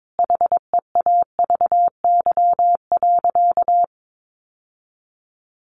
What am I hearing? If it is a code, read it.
Morse code, heard as a keyed tone, about 22 wpm, 700 Hz. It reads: HEAVY.
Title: Sentences from Top 1100 Words 22wpm